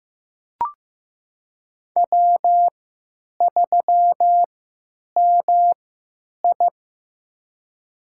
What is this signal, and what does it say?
Morse code, heard as a keyed tone, about 15 wpm, spaced as for 9 wpm, 700 Hz. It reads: W3MI
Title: USA Call Signs - 4 Characters - 3x Character Spacing - 15wpm